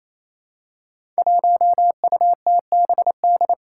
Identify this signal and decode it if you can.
Morse code, heard as a keyed tone, about 27 wpm, 700 Hz. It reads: 1UTBD
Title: Five Challenging Letter-Number Combinations 28wpm